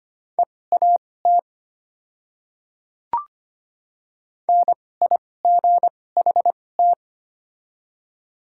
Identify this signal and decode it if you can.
Morse code, heard as a keyed tone, about 25 wpm, spaced as for 18 wpm, 700 Hz. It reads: EAT NIGHT
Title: 2x Character Spacing - Top 500 Words - 25wpm